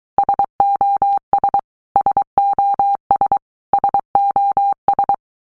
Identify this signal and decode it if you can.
Morse code, heard as a keyed tone, about 23 wpm, 800 Hz. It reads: SOS SOS SOS
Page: SOS sound in Morse Code (Fast)